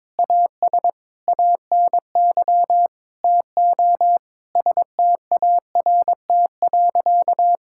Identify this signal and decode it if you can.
Morse code, heard as a keyed tone, about 22 wpm, 700 Hz. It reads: AS ANY TO START.
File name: practice / 22wpm / Sentences with 8-Word Phrases - 22wpm